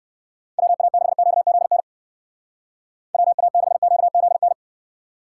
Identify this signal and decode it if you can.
Morse code, heard as a keyed tone, about 59 wpm, 700 Hz. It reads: WA6QZN WA6QZN